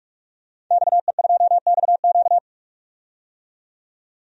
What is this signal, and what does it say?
Morse code, heard as a keyed tone, about 46 wpm, 700 Hz. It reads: XE2XQ
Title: DXCC-100 Call Signs - 5 Characters - 45wpm